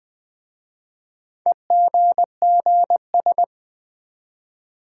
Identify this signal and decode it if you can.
Morse code, heard as a keyed tone, about 20 wpm, 700 Hz. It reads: EGGS
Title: Easy Words with Hints - 20wpm